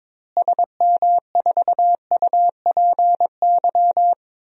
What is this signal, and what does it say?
Morse code, heard as a keyed tone, about 22 wpm, 700 Hz. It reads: SM4UPY
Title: DXCC-100 Call Signs - 6 Characters - Repeated 2x - 22wpm